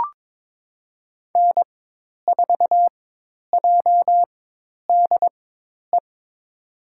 Morse code, heard as a keyed tone, about 22 wpm, spaced as for 10 wpm, 700 Hz. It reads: N4JDE
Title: USA Call Signs - 5 Characters - 4x Character Spacing - 22wpm